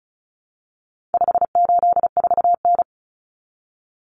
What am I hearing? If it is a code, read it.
Morse code, heard as a keyed tone, about 34 wpm, 700 Hz. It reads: 584N
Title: Common Cut Numbers - 4-Digit 35wpm